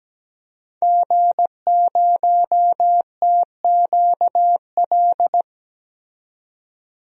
Morse code, heard as a keyed tone, about 17 wpm, 700 Hz. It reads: G0TQL
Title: DXCC-100 Call Signs - 5 Characters - 17wpm